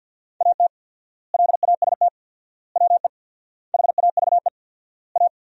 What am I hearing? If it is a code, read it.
Morse code, heard as a keyed tone, about 52 wpm, 700 Hz. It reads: AT LAST WE HAVE A
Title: 4x Word-Spacing - Sentences from Top 300 Words - 50wpm